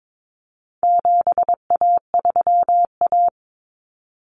Morse code, heard as a keyed tone, about 22 wpm, 700 Hz. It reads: 7A3A
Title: Common Cut Numbers - 4-Digit 22wpm